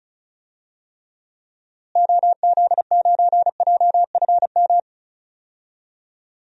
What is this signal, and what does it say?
Morse code, heard as a keyed tone, about 36 wpm, 700 Hz. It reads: OZ9JFM
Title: DXCC-100 Call Signs - 6 Characters - Repeated 2x - 35wpm